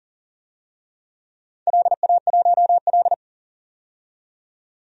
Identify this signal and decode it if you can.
Morse code, heard as a keyed tone, about 40 wpm, 700 Hz. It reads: LA1L
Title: DXCC-100 Call Signs - 4 Characters - 40wpm